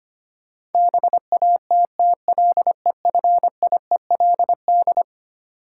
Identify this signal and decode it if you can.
Morse code, heard as a keyed tone, about 25 wpm, 700 Hz. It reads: BATTLEFIELD
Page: morse-code-ninja-practice-25wpm